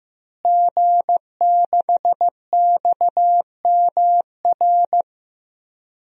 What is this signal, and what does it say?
Morse code, heard as a keyed tone, about 15 wpm, 700 Hz. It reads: G6XMR
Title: DXCC-100 Call Signs - 5 Characters - Repeated 2x - 15wpm